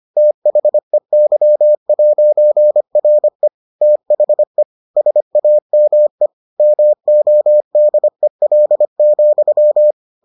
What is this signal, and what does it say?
Morse code, heard as a keyed tone, about 25 wpm, 600 Hz. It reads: THEY'RE THE SAME MODEL,